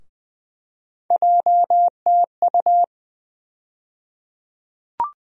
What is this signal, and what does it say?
Morse code, heard as a keyed tone, about 20 wpm, 700 Hz. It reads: JTU
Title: USA Call Sign Postfixes - 3 Letters 20wpm